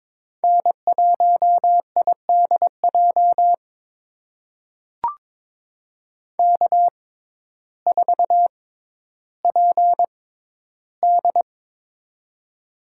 Morse code, heard as a keyed tone, about 22 wpm, spaced as for 12 wpm, 700 Hz. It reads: N1IDJ K4PD
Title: USA Call Signs - 6x Character Spacing - 22wpm